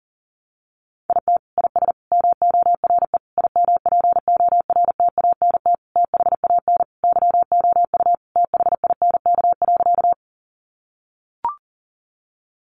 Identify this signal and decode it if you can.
Morse code, heard as a keyed tone, about 37 wpm, 700 Hz. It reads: IT IS MORE IMPORTANT THAN YOU THINK.
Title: Sentences from Top 600 Words 40wpm